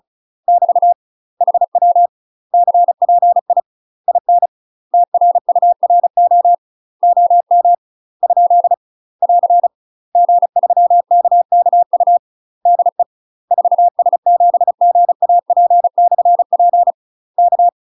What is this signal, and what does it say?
Morse code, heard as a keyed tone, about 36 wpm, 700 Hz. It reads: <BT> HW CPI IN TRURO OM ? <AR> G3KKU DE 4S7GAP/P K
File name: UK MASTER QSOs TRAINING / 35WPM / UK MASTER QSOs 0007